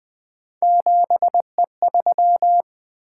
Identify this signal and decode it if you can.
Morse code, heard as a keyed tone, about 20 wpm, 700 Hz. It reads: 7E3
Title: Three Letter-Number Combinations - 20wpm